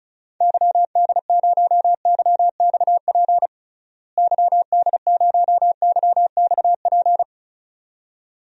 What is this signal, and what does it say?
Morse code, heard as a keyed tone, about 34 wpm, 700 Hz. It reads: YD0YXP YD0YXP
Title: DXCC-100 Call Signs - 6 Characters - Repeated 2x - 35wpm